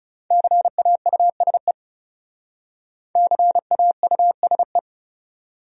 Morse code, heard as a keyed tone, about 38 wpm, 700 Hz. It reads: CAUSE CAUSE